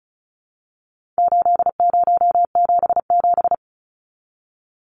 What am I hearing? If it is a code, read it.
Morse code, heard as a keyed tone, about 34 wpm, 700 Hz. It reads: 8077
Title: Numbers - 4-Digit 35wpm